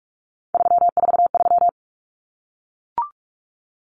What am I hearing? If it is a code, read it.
Morse code, heard as a keyed tone, about 44 wpm, 700 Hz. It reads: 343
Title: Common Cut Numbers - 3-Digit 45wpm